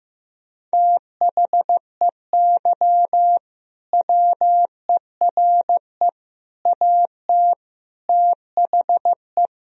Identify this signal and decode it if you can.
Morse code, heard as a keyed tone, about 15 wpm, 700 Hz. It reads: THEY WERE AT THE
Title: Sentences from Top 1000 Words - Review Part 3 - 15wpm